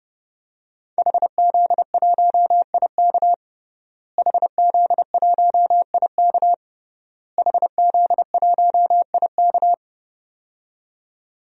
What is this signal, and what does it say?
Morse code, heard as a keyed tone, about 32 wpm, 700 Hz. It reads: HZ1IK HZ1IK HZ1IK
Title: DXCC-100 Call Signs - 5 Characters - Repeated 3x - 30wpm